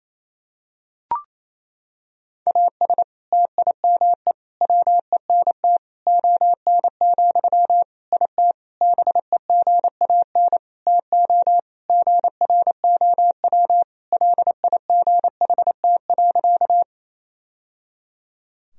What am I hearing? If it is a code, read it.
Morse code, heard as a keyed tone, about 28 wpm, 700 Hz. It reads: AS TIME WENT ON, IT BEGAN TO GROW LIGHT.
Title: Sentences from Top 1000 Words - Review Part 1 - 28wpm